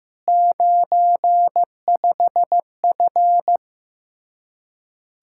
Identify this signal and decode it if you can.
Morse code, heard as a keyed tone, about 15 wpm, 700 Hz. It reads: 95F
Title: Practice QSO - Exchange 2 - Info - 15wpm